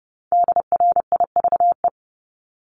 Morse code, heard as a keyed tone, about 30 wpm, 700 Hz. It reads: DRIVE
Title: Multi-letter Phonemes Repeated 3x 30wpm